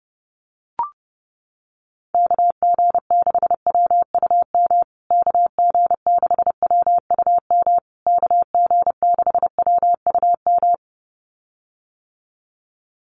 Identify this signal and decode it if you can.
Morse code, heard as a keyed tone, about 30 wpm, 700 Hz. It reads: KG6WUM KG6WUM KG6WUM
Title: USA Call Signs Repeated 3x 30wpm